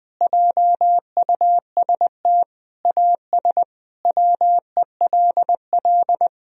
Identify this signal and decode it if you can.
Morse code, heard as a keyed tone, about 20 wpm, 700 Hz. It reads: JUST AS WELL
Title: Sentences from Top 1000 Words - Review Part 2 - 20wpm